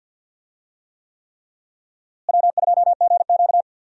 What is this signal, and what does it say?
Morse code, heard as a keyed tone, about 51 wpm, 700 Hz. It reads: W2GQ
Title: USA Call Signs - 4 Characters - Repeated 3x - 50wpm